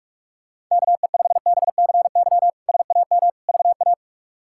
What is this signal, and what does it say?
Morse code, heard as a keyed tone, about 46 wpm, 700 Hz. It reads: KE5BCY SAM VA
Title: K1USN SST QSO Practice - 45wpm